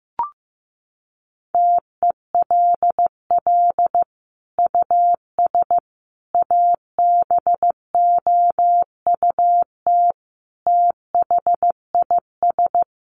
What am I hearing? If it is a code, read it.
Morse code, heard as a keyed tone, about 15 wpm, 700 Hz. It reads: TELL US ABOUT THIS
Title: Sentences from Top 1200 Words 15wpm